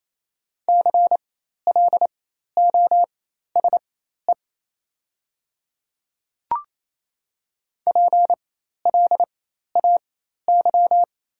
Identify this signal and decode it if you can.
Morse code, heard as a keyed tone, about 28 wpm, spaced as for 13 wpm, 700 Hz. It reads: CLOSE PLAY
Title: 4x Character Spacing - Top 300 Words - 28wpm